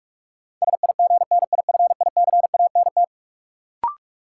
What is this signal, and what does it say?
Morse code, heard as a keyed tone, about 46 wpm, 700 Hz. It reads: SIGNIFICANT
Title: Sentences from Top 1600 Words - New Word Drill Alt - 45wpm